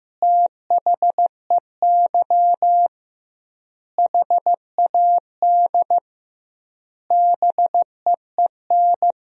Morse code, heard as a keyed tone, about 15 wpm, 700 Hz. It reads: THEY HAD BEEN
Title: 2x Word-Spacing - Sentences from Top 400 Words - 15wpm